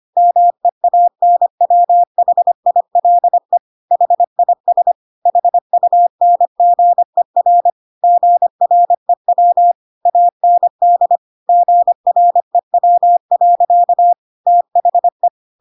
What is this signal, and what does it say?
Morse code, heard as a keyed tone, about 25 wpm, 700 Hz. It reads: MEANWHILE HIS HUNGER GREW AND GREW. THE
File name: Pinocchio / Pinocchio ch5 25 25 ik1hge